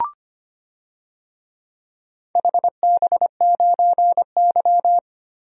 Morse code, heard as a keyed tone, about 25 wpm, 700 Hz. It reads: HB9Y
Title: DXCC-100 Call Signs - 4 Characters - 25wpm